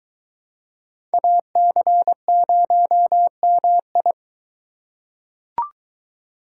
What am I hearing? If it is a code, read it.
Morse code, heard as a keyed tone, about 23 wpm, 700 Hz. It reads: AC0MI